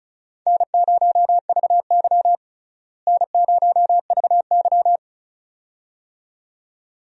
Morse code, heard as a keyed tone, about 34 wpm, 700 Hz. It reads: N0VY N0VY
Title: USA Call Signs - 4 Characters - Repeated 2x - 35wpm